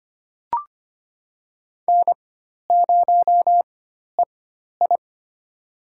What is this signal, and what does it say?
Morse code, heard as a keyed tone, about 25 wpm, spaced as for 12 wpm, 700 Hz. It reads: N0EI
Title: USA Call Signs - 4 Characters - 4x Character Spacing - 25wpm